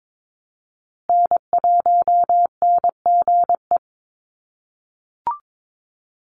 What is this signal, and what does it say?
Morse code, heard as a keyed tone, about 22 wpm, 700 Hz. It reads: N1NGE
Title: USA Call Signs Repeated 3x 22wpm